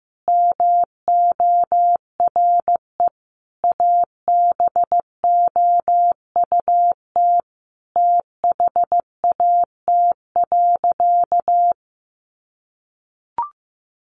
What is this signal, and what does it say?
Morse code, heard as a keyed tone, about 14 wpm, 700 Hz. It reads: MORE ABOUT THAT.
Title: Sentences from Top 200 Words 15wpm